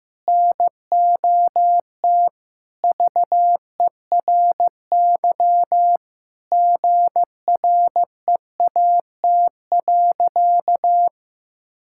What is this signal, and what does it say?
Morse code, heard as a keyed tone, about 15 wpm, 700 Hz. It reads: NOT VERY GREAT.
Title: Sentences from Top 1000 Words - Review Part 5 - 15wpm